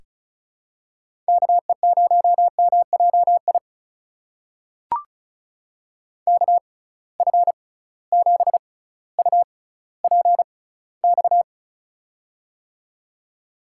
Morse code, heard as a keyed tone, about 34 wpm, spaced as for 18 wpm, 700 Hz. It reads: KE0MJI KF7UPX
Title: USA Call Signs - 6 Characters - 6x Character Spacing - 35wpm